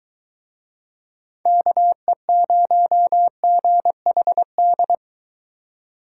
Morse code, heard as a keyed tone, about 22 wpm, 700 Hz. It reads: KE0GHD